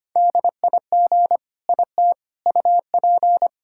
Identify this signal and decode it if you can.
Morse code, heard as a keyed tone, about 25 wpm, 700 Hz. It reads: DIG IT UP
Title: Sentences from Top 1600 Words 25wpm